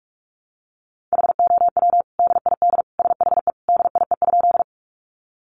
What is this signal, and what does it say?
Morse code, heard as a keyed tone, about 44 wpm, 700 Hz. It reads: HOW DID SHE DIE?
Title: Sentences from Top 1300 Words 45wpm